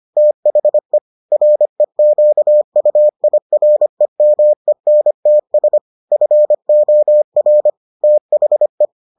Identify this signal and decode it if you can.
Morse code, heard as a keyed tone, about 25 wpm, 600 Hz. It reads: THE REQUIREMENTS FOR THE